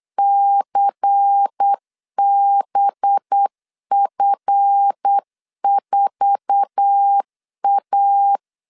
Morse code, heard as a keyed tone, about 8 wpm, 800 Hz. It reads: CBF4A
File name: LEZIONI / 9 286 8wpm 3425